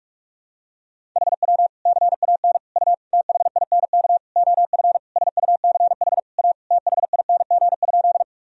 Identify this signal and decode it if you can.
Morse code, heard as a keyed tone, about 41 wpm, 700 Hz. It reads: HW CAN U THINK OF SUCH A THING?
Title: Sentences from Top 200 Words and Abbreviations 45wpm